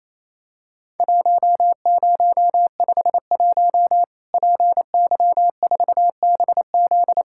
Morse code, heard as a keyed tone, about 28 wpm, 700 Hz. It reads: 1051 PY4BZ
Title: Worked All Europe DX Contest - QTC Exchange - 28wpm